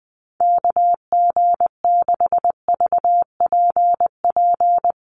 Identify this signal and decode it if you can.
Morse code, heard as a keyed tone, about 19 wpm, 700 Hz. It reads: KG6VPP
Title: USA Call Signs Repeated 3x 20wpm